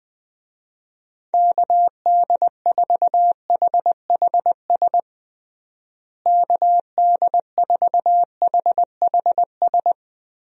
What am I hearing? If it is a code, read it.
Morse code, heard as a keyed tone, about 20 wpm, 700 Hz. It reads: KD4HHS KD4HHS
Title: USA Call Signs - 6 Characters - Repeated 2x - 20wpm